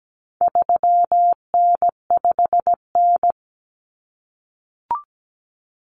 Morse code, heard as a keyed tone, about 17 wpm, 700 Hz. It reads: 3N5N
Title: Common Cut Numbers - 4-Digit 17wpm